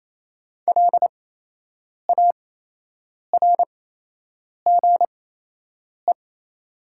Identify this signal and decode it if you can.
Morse code, heard as a keyed tone, about 28 wpm, spaced as for 8 wpm, 700 Hz. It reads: LARGE
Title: 8x Character Spacing - Top 500 Words - 28wpm